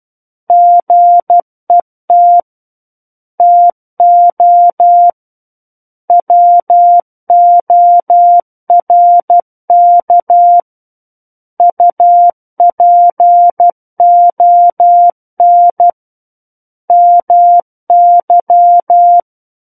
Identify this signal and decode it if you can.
Morse code, heard as a keyed tone, about 12 wpm, 700 Hz. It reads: GET TO WORK UPON MY
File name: Book One / 03-Chapter 3 - 12 WPM